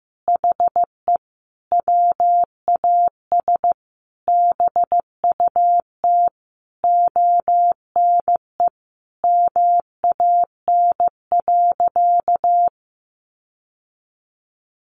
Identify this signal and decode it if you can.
Morse code, heard as a keyed tone, about 15 wpm, 700 Hz. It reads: HE WAS BUT ONE MAN.
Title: Sentences from Top 1500 Words 15wpm